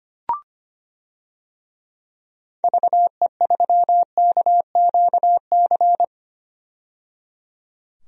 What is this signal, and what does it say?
Morse code, heard as a keyed tone, about 25 wpm, 700 Hz. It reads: VE3KQC
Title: DXCC-100 Call Signs - 6 Characters - 25wpm